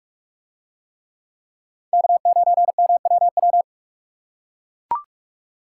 Morse code, heard as a keyed tone, about 41 wpm, 700 Hz. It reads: K9MWW